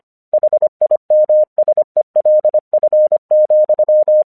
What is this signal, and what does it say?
Morse code, heard as a keyed tone, about 24 wpm, 610 Hz. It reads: HIMSELF,
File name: 1 English 25wpm / AQC26